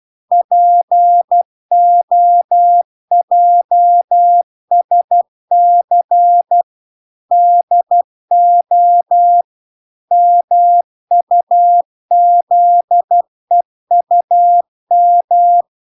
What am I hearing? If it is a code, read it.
Morse code, heard as a keyed tone, about 12 wpm, 700 Hz. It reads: POJSC DO MUZEUM